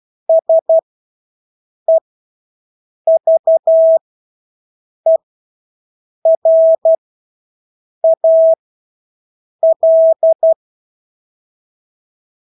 Morse code, heard as a keyed tone, about 12 wpm, spaced as for 6 wpm, 650 Hz. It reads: SEVERAL
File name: Engelske ord 12-6 wpm